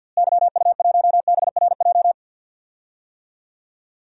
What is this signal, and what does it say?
Morse code, heard as a keyed tone, about 52 wpm, 700 Hz. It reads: YU1BRJ
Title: DXCC-100 Call Signs - 6 Characters - Repeated 2x - 50wpm